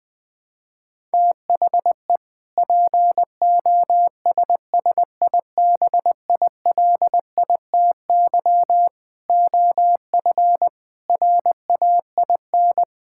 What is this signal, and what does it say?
Morse code, heard as a keyed tone, about 19 wpm, 700 Hz. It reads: THE POSSIBILITY OF RAIN
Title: Sentences with 3-Word Phrases - 20wpm